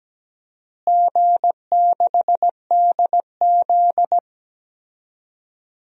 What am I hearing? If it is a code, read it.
Morse code, heard as a keyed tone, about 17 wpm, 700 Hz. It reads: G6DZ
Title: DXCC-100 Call Signs - 4 Characters - Repeated 3x - 17wpm